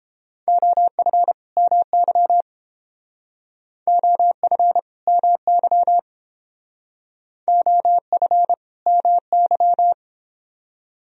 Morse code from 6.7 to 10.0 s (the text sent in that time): OF MY